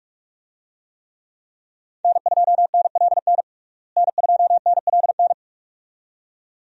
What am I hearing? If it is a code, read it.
Morse code, heard as a keyed tone, about 46 wpm, 700 Hz. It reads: N2NLN N2NLN
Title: USA Call Signs - 5 Characters - Repeated 2x - 45wpm